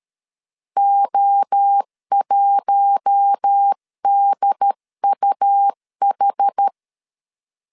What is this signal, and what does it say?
Morse code, heard as a keyed tone, about 13 wpm, 800 Hz. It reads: O1DUH
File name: CW / DET / 1 6 1500C S C 12wpm 3533